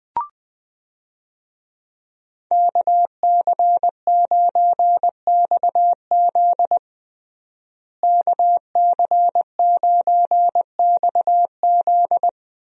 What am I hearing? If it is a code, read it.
Morse code, heard as a keyed tone, about 20 wpm, 700 Hz. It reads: KC9XZ KC9XZ
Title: USA Call Signs - 5 Characters - Repeated 2x - 20wpm